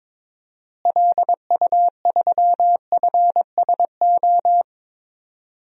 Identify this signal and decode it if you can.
Morse code, heard as a keyed tone, about 22 wpm, 700 Hz. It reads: LU3FSO
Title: DXCC-100 Call Signs - 6 Characters - Repeated 2x - 22wpm